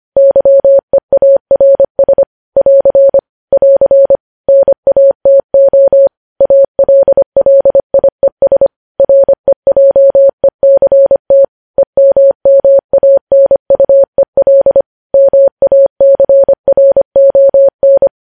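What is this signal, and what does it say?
Morse code, heard as a keyed tone, about 25 wpm, 550 Hz. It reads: YEARS <AR> <AR> NATO ALLIES REJECT EMMANUEL MACRON
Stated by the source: News headlines sent in Morse code at 25 wpm.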